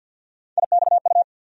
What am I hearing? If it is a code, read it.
Morse code, heard as a keyed tone, about 52 wpm, 700 Hz. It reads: IXU
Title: USA Call Sign Postfixes - 3 Letters 50wpm